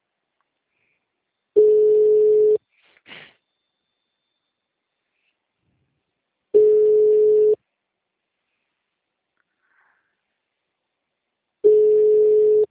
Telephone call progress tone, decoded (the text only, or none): ringback tone